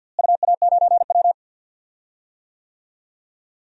Morse code, heard as a keyed tone, about 53 wpm, 700 Hz. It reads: UA9W
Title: DXCC-100 Call Signs - 4 Characters - Repeated 2x - 50wpm